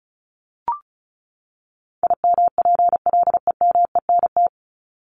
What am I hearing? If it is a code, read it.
Morse code, heard as a keyed tone, about 34 wpm, 700 Hz. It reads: IMPLEMENT
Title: Top 1501-1750 Words 35wpm